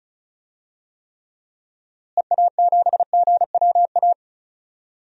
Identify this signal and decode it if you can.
Morse code, heard as a keyed tone, about 36 wpm, 700 Hz. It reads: EA7GWA
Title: DXCC-100 Call Signs - 6 Characters - Repeated 3x - 35wpm